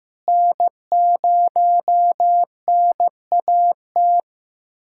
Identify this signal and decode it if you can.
Morse code, heard as a keyed tone, about 15 wpm, 700 Hz. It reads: N0NAT
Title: USA Callsign DE Callsign - 5 Characters - 15wpm